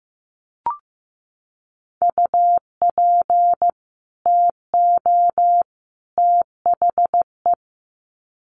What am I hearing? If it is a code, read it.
Morse code, heard as a keyed tone, about 15 wpm, 700 Hz. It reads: UP TO THE
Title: Sets of 3 Words 15wpm